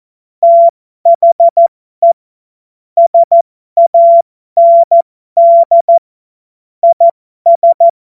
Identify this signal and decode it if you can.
Morse code, heard as a keyed tone, about 14 wpm, spaced as for 12 wpm, 680 Hz. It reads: THE SAND IS